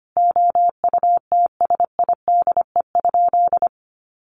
Morse code, heard as a keyed tone, about 25 wpm, 700 Hz. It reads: OUTSIDE?
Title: Sentences from Top 1200 Words 25wpm